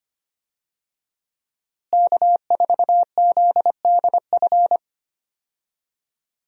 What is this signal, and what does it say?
Morse code, heard as a keyed tone, about 25 wpm, 700 Hz. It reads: K4ZDF
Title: USA Call Signs - 5 Characters - Repeated 3x - 25wpm